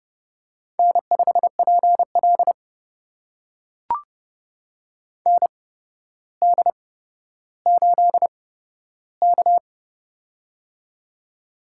Morse code, heard as a keyed tone, about 30 wpm, spaced as for 13 wpm, 700 Hz. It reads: N5PL ND8K
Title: USA Call Signs - 4 Characters - 8x Character Spacing - 30wpm